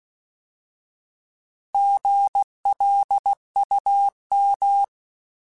Morse code, heard as a keyed tone, about 16 wpm, 800 Hz. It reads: GLUM
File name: Alphabetical / Alphabetical mp3s / Gs @15